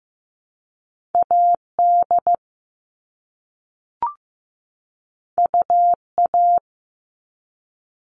Morse code, heard as a keyed tone, about 15 wpm, 700 Hz. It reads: AD UA
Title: All Cut Numbers - 2-Digit 15wpm